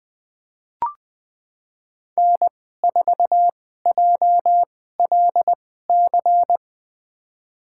Morse code, heard as a keyed tone, about 20 wpm, spaced as for 14 wpm, 700 Hz. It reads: N4JLC
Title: USA Call Signs - 5 Characters - 2x Character Spacing - 20wpm